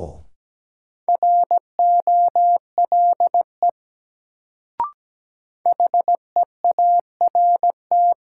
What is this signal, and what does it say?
Morse code, heard as a keyed tone, about 17 wpm, 700 Hz. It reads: ROLE HEART
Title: Sets of 3 Words - New Word Drill - 17wpm